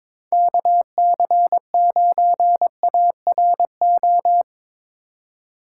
Morse code, heard as a keyed tone, about 22 wpm, 700 Hz. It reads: KC9ARO
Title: USA Call Signs - 6 Characters - 2x Character Spacing - 22wpm